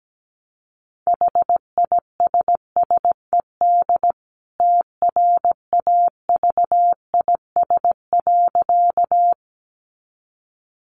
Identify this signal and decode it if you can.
Morse code, heard as a keyed tone, about 17 wpm, 700 Hz. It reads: HISSED TRAVIS.
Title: A Sound Of Thunder 17wpm